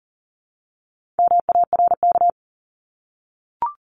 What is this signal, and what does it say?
Morse code, heard as a keyed tone, about 40 wpm, 700 Hz. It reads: MARK
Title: Top 500 Words Repeated 40wpm